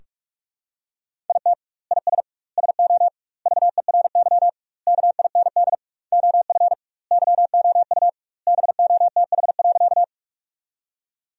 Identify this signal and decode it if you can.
Morse code, heard as a keyed tone, about 43 wpm, 700 Hz. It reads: IT IS SO VERY KIND OF YOU BOTH.
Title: 2x Word-Spacing - Sentences from Top 300 Words - 45wpm